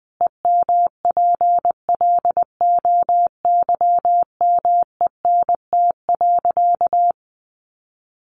Morse code, heard as a keyed tone, about 19 wpm, 700 Hz. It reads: EMPLOYMENT.
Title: Sentences from Top 1500 Words 20wpm